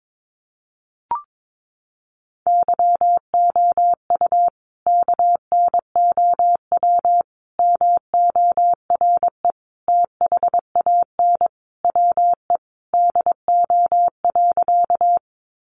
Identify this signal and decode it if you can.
Morse code, heard as a keyed tone, about 22 wpm, 700 Hz. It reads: YOU KNOW MORE THAN WE DO.
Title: Sentences from Top 100 Words 22wpm